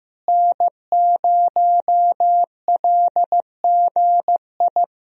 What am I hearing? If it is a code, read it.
Morse code, heard as a keyed tone, about 15 wpm, 700 Hz. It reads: N0LGI
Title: USA Call Signs - 5 Characters - Repeated 2x - 15wpm